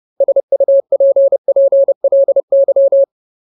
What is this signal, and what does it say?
Morse code, heard as a keyed tone, about 30 wpm, 550 Hz. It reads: SUPPLY
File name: Ultimate Word List / 30WPM Ultimate